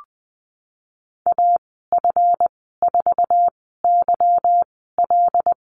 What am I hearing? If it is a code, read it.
Morse code, heard as a keyed tone, about 20 wpm, spaced as for 15 wpm, 700 Hz. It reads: AF4YL
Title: 167 Lesson 27g - Number 7 - Call Signs 20wpm